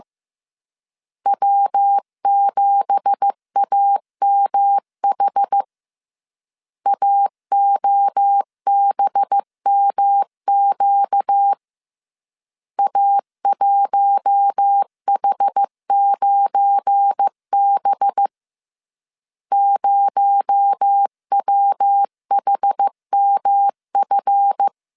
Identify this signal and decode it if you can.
Morse code, heard as a keyed tone, about 15 wpm, 800 Hz. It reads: W7AMH AOBMQ A1H9B 0WHMF
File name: CW / DET / 1 9 1500C S C 14wpm 3245